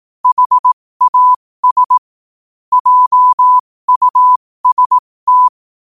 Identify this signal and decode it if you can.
Morse code, heard as a keyed tone, about 18 wpm, spaced as for 15 wpm, 1000 Hz. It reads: HAS JUST
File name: qotdfiles / 15wpm